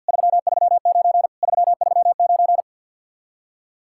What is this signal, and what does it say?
Morse code, heard as a keyed tone, about 61 wpm, 700 Hz. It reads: 339 339
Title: Speed-Racing Common QSO Phrases - 50wpm